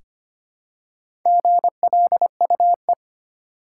Morse code, heard as a keyed tone, about 25 wpm, 700 Hz. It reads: GLUE
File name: practice / 25wpm / Easy Words with Hints - 25wpm